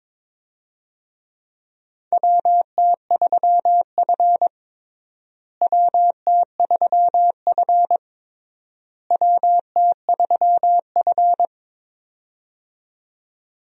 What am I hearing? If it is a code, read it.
Morse code, heard as a keyed tone, about 22 wpm, 700 Hz. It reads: WT3F WT3F WT3F
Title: USA Call Signs - 4 Characters - Repeated 3x - 22wpm